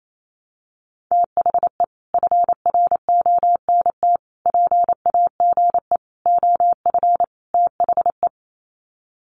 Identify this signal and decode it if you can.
Morse code, heard as a keyed tone, about 28 wpm, 700 Hz. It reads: THE FRONT PAGE OF THE
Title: Sets of 5 Words Encore 28wpm